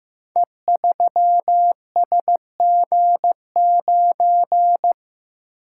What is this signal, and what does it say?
Morse code, heard as a keyed tone, about 15 wpm, 700 Hz. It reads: E3SG9
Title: Code Groups - 5 Letter:Numbers - 15wpm